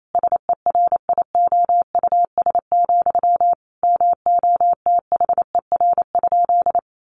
Morse code, heard as a keyed tone, about 28 wpm, 700 Hz. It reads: SERIOUS, MOTHER?
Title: Sentences from Top 1100 Words 28wpm